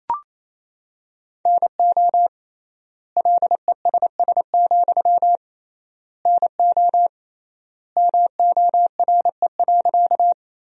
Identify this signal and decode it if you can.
Morse code, heard as a keyed tone, about 28 wpm, 700 Hz. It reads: NO LESS, NO MORE.
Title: 3x Word-Spacing - Sentences from Top 400 Words - 28wpm